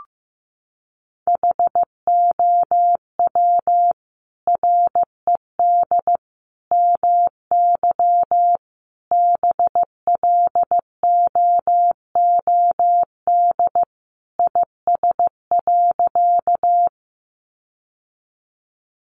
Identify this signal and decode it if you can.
Morse code, heard as a keyed tone, about 15 wpm, 700 Hz. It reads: HOW RED MY BLOOD IS.
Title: Sentences from Top 1100 Words 15wpm